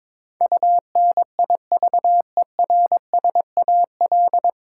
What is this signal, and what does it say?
Morse code, heard as a keyed tone, about 22 wpm, 700 Hz. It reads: UNIVERSAL
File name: 9 Letter Words - First 3 Letters Given - 22wpm